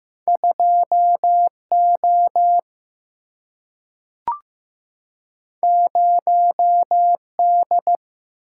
Morse code, heard as a keyed tone, about 15 wpm, 700 Hz. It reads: 2 O 0 D
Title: Single-Letter Single-Number 15wpm